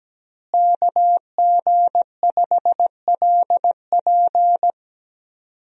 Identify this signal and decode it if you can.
Morse code, heard as a keyed tone, about 17 wpm, 700 Hz. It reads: KG5LP
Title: USA Call Signs - 5 Characters - Repeated 3x - 17wpm